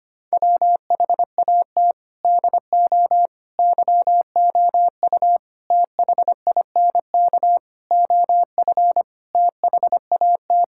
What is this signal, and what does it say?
Morse code, heard as a keyed tone, about 25 wpm, 700 Hz. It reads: WHAT DO YOU THINK OF THAT
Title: Sentences from Top 1000 Words - Review Part 6 - 25wpm